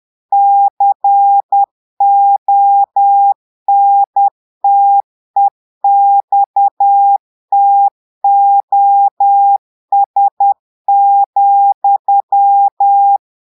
Morse code, heard as a keyed tone, about 10 wpm, 800 Hz. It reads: CONTEXTOS,
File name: Portugais - texte en clair 1 - vitesse - 10 mots minute